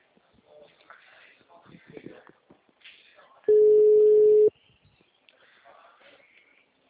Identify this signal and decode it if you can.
Ringback tone (Europe, 425 Hz), with speech.
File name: out- -2005- - - .17775